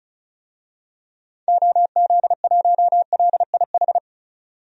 Morse code, heard as a keyed tone, about 36 wpm, 700 Hz. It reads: OZ1LIH
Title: DXCC-100 Call Signs - 6 Characters - 35wpm